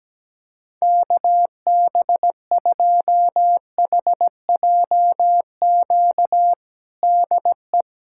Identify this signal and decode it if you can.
Morse code, heard as a keyed tone, about 17 wpm, 700 Hz. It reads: KB2HJQ DE
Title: USA Callsign DE Callsign 17wpm